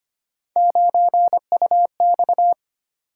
Morse code, heard as a keyed tone, about 25 wpm, 700 Hz. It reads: 9UX
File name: Three Letter-Number Combinations - 25wpm